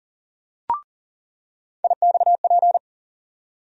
Morse code, heard as a keyed tone, about 40 wpm, 700 Hz. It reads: IXP
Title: USA Call Sign Postfixes - 3 Letters 40wpm